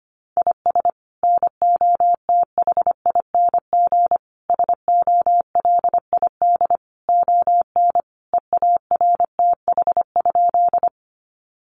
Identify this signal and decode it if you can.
Morse code, heard as a keyed tone, about 25 wpm, 700 Hz. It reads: IS NOTHING SOLID ON EARTH?
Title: Sentences from Top 1500 Words 25wpm